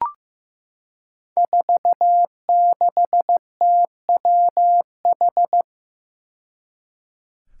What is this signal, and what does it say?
Morse code, heard as a keyed tone, about 15 wpm, 700 Hz. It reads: 46TWH
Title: Five Challenging Letter-Number Combinations 15wpm